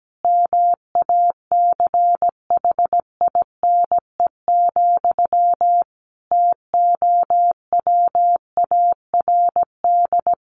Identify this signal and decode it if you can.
Morse code, heard as a keyed tone, about 17 wpm, 700 Hz. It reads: MACHINE, TOWARD
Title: A Sound Of Thunder 17wpm